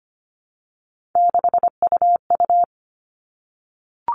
Morse code, heard as a keyed tone, about 25 wpm, 700 Hz. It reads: 6UU
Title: All Cut Numbers - 3-Digit 25wpm